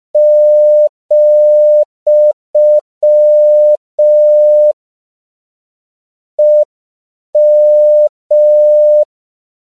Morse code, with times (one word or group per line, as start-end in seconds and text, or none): 0.2-4.7 ,
6.4-9.0 EM